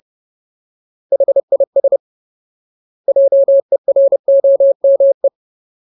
Morse code, heard as a keyed tone, about 30 wpm, 550 Hz. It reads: HIS JEROME
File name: Ultimate Word List / 30WPM Ultimate